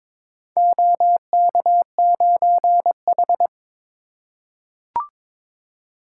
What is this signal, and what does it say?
Morse code, heard as a keyed tone, about 21 wpm, 700 Hz. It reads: OK9H
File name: practice / 22wpm / Four Letter-Number Combinations - 22wpm